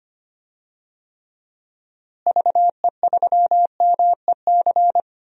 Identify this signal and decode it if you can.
Morse code, heard as a keyed tone, about 25 wpm, 700 Hz. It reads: VE3MEC
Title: DXCC-100 Call Signs - 6 Characters - Repeated 3x - 25wpm